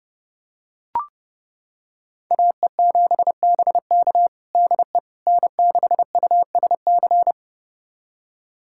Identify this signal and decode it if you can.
Morse code, heard as a keyed tone, about 30 wpm, 700 Hz. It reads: AE7BK DE N6USC
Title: USA Callsign DE Callsign - 5 Characters - 30wpm